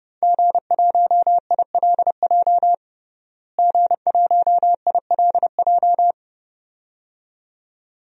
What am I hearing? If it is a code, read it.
Morse code, heard as a keyed tone, about 30 wpm, 700 Hz. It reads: G1ILJ G1ILJ
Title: DXCC-100 Call Signs - 5 Characters - Repeated 2x - 30wpm